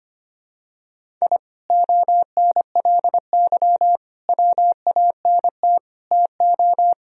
Morse code, heard as a keyed tone, about 25 wpm, 700 Hz. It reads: I ONLY WANT TO
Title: Sentences from Top 1000 Words - Review Part 1 - 25wpm